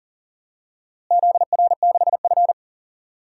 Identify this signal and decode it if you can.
Morse code, heard as a keyed tone, about 37 wpm, 700 Hz. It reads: ZR6F
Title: DXCC-100 Call Signs - 4 Characters - Repeated 2x - 40wpm